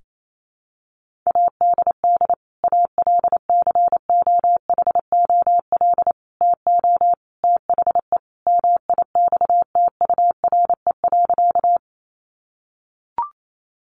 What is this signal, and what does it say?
Morse code, heard as a keyed tone, about 28 wpm, 700 Hz. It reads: ADD ALCOHOL TO THE MIXTURE.
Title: Sentences from Top 1500 Words 28wpm